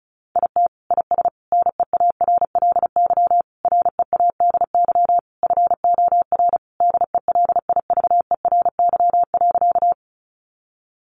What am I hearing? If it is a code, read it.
Morse code, heard as a keyed tone, about 34 wpm, 700 Hz. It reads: IT IS NEARLY READY FOR DELIVERY.
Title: Sentences from Top 1500 Words 35wpm